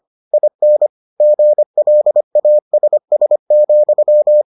Morse code, heard as a keyed tone, about 25 wpm, 600 Hz. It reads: IN GLASS,